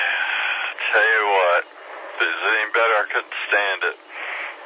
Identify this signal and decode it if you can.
Emergency Alert System (SAME) tones, heard as harsh buzzing bursts, with speech. It is heard from 0.8 to 4.0 s.